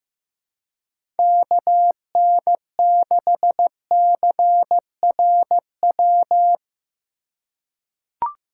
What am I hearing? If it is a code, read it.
Morse code, heard as a keyed tone, about 16 wpm, 700 Hz. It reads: KN6CRW
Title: USA Call Signs - 6 Characters - 2x Character Spacing - 15wpm